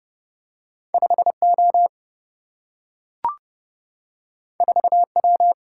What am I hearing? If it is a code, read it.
Morse code, heard as a keyed tone, about 30 wpm, 700 Hz. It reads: 5 O 4 W
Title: Single-Letter Single-Number 30wpm